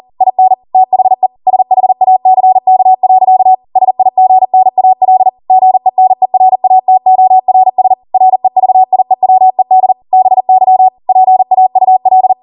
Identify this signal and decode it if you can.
Morse code, heard as a keyed tone, about 40 wpm, 750 Hz. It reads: IN THE SHACK. SIGNAL GENERATORS REVIEWED BY PAUL